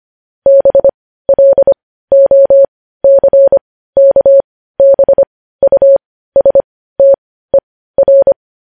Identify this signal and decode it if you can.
Morse code, heard as a keyed tone, about 25 wpm, spaced as for 15 wpm, 550 Hz. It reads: BLOCKBUSTER
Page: news headlines in morse code at 15 wpm